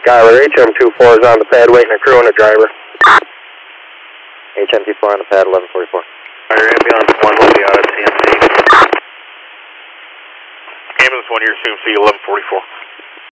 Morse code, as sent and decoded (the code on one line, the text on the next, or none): none